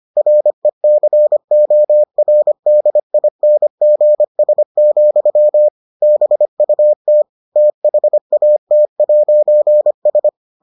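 Morse code, heard as a keyed tone, about 25 wpm, 600 Hz. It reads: RECORDINGS, BUT THAT'S
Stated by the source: Morse edition of the Foundations of Amateur Radio episodes.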